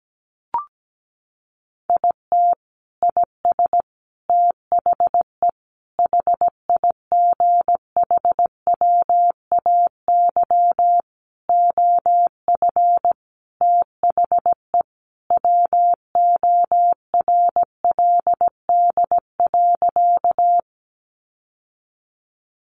Morse code, heard as a keyed tone, about 17 wpm, 700 Hz. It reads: IT IS THE HIGHWAY OF THE WORLD.
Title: Sentences from Top 1400 Words 17wpm